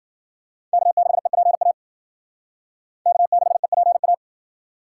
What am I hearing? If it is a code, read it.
Morse code, heard as a keyed tone, about 51 wpm, 700 Hz. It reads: K6EPA K6EPA